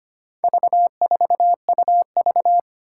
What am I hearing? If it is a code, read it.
Morse code, heard as a keyed tone, about 25 wpm, 700 Hz. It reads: V4UV
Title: Four Challenging Letter-Number Combinations 25wpm